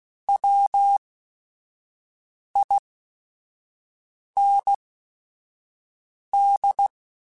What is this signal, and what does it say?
Morse code, heard as a keyed tone, about 16 wpm, spaced as for 5 wpm, 800 Hz. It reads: WIND